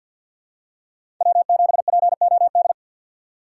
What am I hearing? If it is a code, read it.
Morse code, heard as a keyed tone, about 50 wpm, 700 Hz. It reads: W7POD
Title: USA Call Signs - 5 Characters - 2x Character Spacing - 50wpm